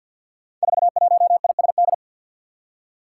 Morse code, heard as a keyed tone, about 48 wpm, 700 Hz. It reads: 41ISD
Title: Five Challenging Letter-Number Combinations 50wpm